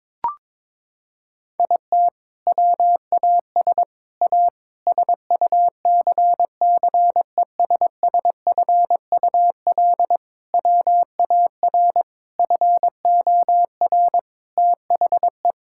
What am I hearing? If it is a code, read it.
Morse code, heard as a keyed tone, about 22 wpm, 700 Hz. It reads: IT WAS A SUCCESSFUL WAR FOR THE
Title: Sentences from Top 1000 Words - Review Part 6 - 22wpm